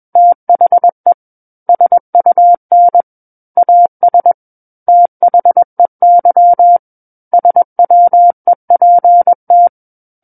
Morse code, heard as a keyed tone, about 21 wpm, 700 Hz. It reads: THE SUN AS THEY SWEPT
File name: Book One / 12-Chapter 12 - 21 WPM